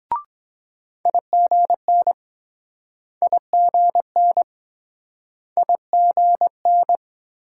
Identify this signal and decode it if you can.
Morse code, heard as a keyed tone, about 23 wpm, 700 Hz. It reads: IGN IGN IGN